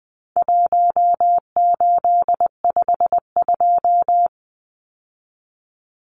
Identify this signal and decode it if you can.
Morse code, heard as a keyed tone, about 20 wpm, 700 Hz. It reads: 1852
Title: Numbers - 4-Digit 20wpm